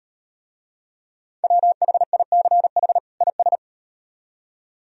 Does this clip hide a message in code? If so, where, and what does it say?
Yes: Morse code from 1.4 to 3.6 s, heard as a keyed tone, about 38 wpm, 700 Hz. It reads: WHICH IS